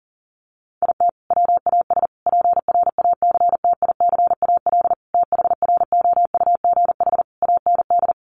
Morse code, heard as a keyed tone, about 40 wpm, 700 Hz. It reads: IT WAS PRACTICAL THROUGH AND
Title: Sentences from Top 1400 Words 40wpm